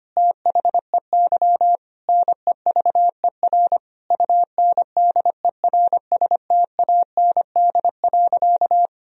Morse code, heard as a keyed tone, about 25 wpm, 700 Hz. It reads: THEY NEVER UNDERSTAND.
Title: Sentences from Top 1000 Words - Review Part 3 - 25wpm